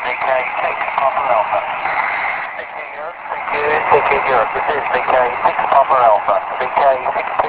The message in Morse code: none